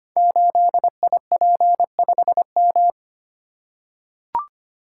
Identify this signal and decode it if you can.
Morse code, heard as a keyed tone, about 25 wpm, 700 Hz. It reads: 8IP5M
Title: Code Groups - 5 Letter:Numbers - 25wpm